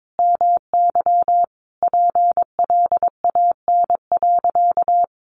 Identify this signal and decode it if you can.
Morse code, heard as a keyed tone, about 22 wpm, 700 Hz. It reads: MY PLAN.
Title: Sentences from Top 500 Words 22wpm